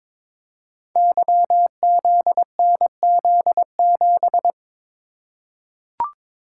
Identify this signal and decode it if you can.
Morse code, heard as a keyed tone, about 22 wpm, 700 Hz. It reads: YZNZ7
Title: Code Groups - 5 Letter:Numbers - 22wpm